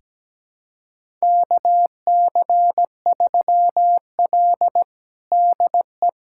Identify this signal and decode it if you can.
Morse code, heard as a keyed tone, about 17 wpm, 700 Hz. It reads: KC3L DE
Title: USA Callsign DE Callsign - 4 Characters - 17wpm